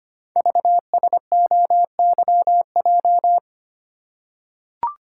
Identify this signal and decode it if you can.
Morse code, heard as a keyed tone, about 25 wpm, 700 Hz. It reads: VSOYJ
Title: Code Groups - 5 Letters - 25wpm